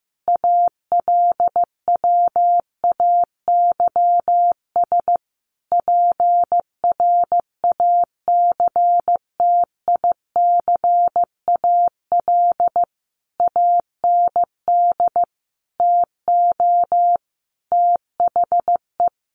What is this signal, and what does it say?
Morse code, heard as a keyed tone, about 15 wpm, 700 Hz. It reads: ALWAYS PRACTICAL AND TO THE
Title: Sentences from Top 1400 Words 15wpm